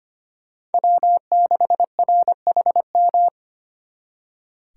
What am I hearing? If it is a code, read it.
Morse code, heard as a keyed tone, about 31 wpm, 700 Hz. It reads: W6RHM
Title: Speed Racing - USA 1x3 Call Signs 25wpm